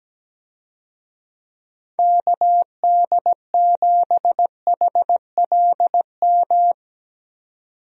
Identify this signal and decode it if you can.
Morse code, heard as a keyed tone, about 17 wpm, 700 Hz. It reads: KD7HLM
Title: USA Call Signs - 6 Characters - 17wpm